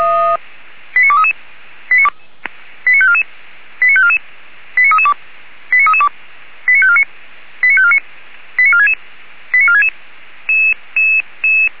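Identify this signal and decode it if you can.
Five-tone selective call (ZVEI), heard as quick runs of five tones, tones of about 70 ms. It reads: 89233 89544 89544 89332 89332 89549 89549 89466 89466
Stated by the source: ZVEI five-tone selective call received on SDR.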